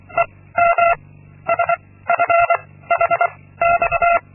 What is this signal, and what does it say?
Morse code, heard as a keyed tone, about 25 wpm, 680 Hz. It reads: E M S F H X
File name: Linksmakalnis-2